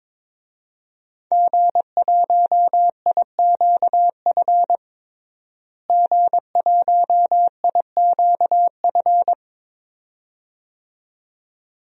Morse code, heard as a keyed tone, about 22 wpm, 700 Hz. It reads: G1IQF G1IQF
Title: DXCC-100 Call Signs - 5 Characters - Repeated 2x - 22wpm